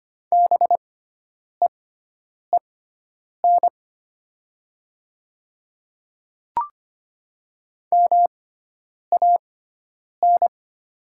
Morse code, heard as a keyed tone, about 25 wpm, spaced as for 9 wpm, 700 Hz. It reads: BEEN MAN
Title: 6x Character Spacing - Top 200 Words - 25wpm